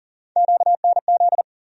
Morse code, heard as a keyed tone, about 40 wpm, 700 Hz. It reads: QNZ
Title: USA Call Sign Postfixes - 3 Letters 40wpm